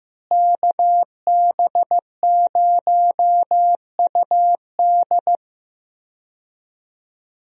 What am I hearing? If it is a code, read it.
Morse code, heard as a keyed tone, about 15 wpm, 700 Hz. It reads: KB0UD
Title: USA Call Signs - 5 Characters - Repeated 3x - 15wpm